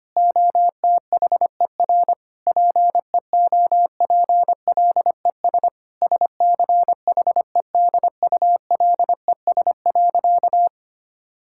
Morse code, heard as a keyed tone, about 25 wpm, 700 Hz. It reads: OTHER PEOPLES SCHEDULES.
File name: practice / 25wpm / Difficult Sentences with Hints - 25wpm